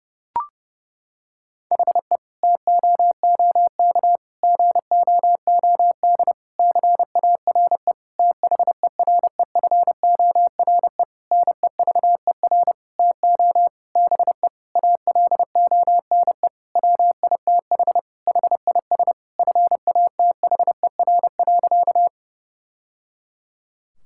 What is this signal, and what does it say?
Morse code, heard as a keyed tone, about 30 wpm, 700 Hz. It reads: HE TOOK GOOD CARE THEREFORE NEVER TO BE ALONE WITH HIS FATHER.
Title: Sentences from Top 1000 Words - Review Part 5 - 30wpm